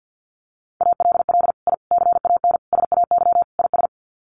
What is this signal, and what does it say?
Morse code, heard as a keyed tone, about 50 wpm, 700 Hz. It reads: ALL I CAN SAY IS
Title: Sets of 5 Words Encore2 50wpm